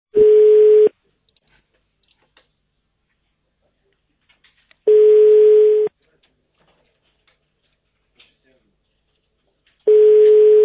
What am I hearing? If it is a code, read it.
Ringback tone (Europe, 425 Hz), with speech.